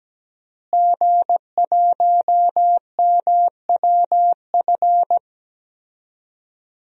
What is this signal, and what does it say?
Morse code, heard as a keyed tone, about 17 wpm, 700 Hz. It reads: G1MWF
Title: DXCC-100 Call Signs - 5 Characters - Repeated 3x - 17wpm